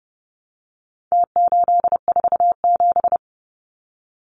Morse code, heard as a keyed tone, about 28 wpm, 700 Hz. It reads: T847
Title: Numbers - Contest Style - 4-Digit 30wpm